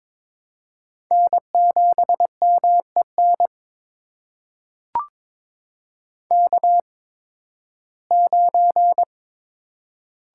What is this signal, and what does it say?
Morse code, heard as a keyed tone, about 22 wpm, spaced as for 14 wpm, 700 Hz. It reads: N7MEN K9
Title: USA Call Signs - 5 Characters - 8x Character Spacing - 22wpm